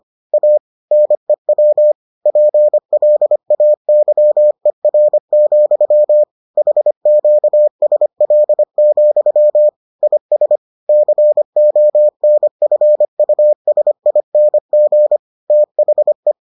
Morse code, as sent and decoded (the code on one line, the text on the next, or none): .- / -. .-- / .--. .-.. .- -.-- . .-. --..-- / .... --.- ... .-.. --..-- / .. ... / -.-. --- -. ..-. ..- ... .. -. --. / - .... .
A NEW PLAYER, HQSL, IS CONFUSING THE